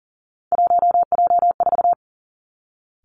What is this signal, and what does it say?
Morse code, heard as a keyed tone, about 40 wpm, 700 Hz. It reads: 1J4
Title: Three Challenging Letter-Number Combinations 40wpm